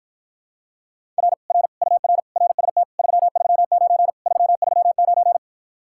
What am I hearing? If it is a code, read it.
Morse code, heard as a keyed tone, about 54 wpm, 700 Hz. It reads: R R UR RST 339 339